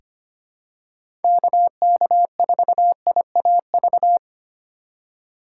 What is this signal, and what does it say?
Morse code, heard as a keyed tone, about 25 wpm, 700 Hz. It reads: KK4IAV
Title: USA Call Signs - 6 Characters - 3x Character Spacing - 25wpm